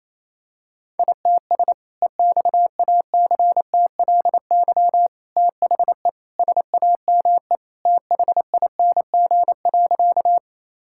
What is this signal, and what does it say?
Morse code, heard as a keyed tone, about 28 wpm, 700 Hz. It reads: ITS EXACTLY THE SAME THING.
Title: Sentences from Top 1000 Words - Review Part 6 - 28wpm